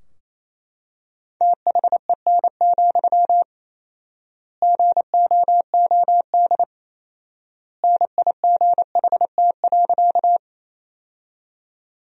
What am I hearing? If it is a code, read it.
Morse code, heard as a keyed tone, about 28 wpm, 700 Hz. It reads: THEN, GOOD NIGHT.
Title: 4x Word-Spacing - Sentences from Top 300 Words - 28wpm